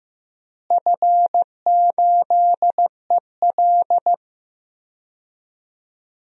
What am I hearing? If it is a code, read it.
Morse code, heard as a keyed tone, about 15 wpm, 700 Hz. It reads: F8EL
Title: DXCC-100 Call Signs - 4 Characters - Repeated 3x - 15wpm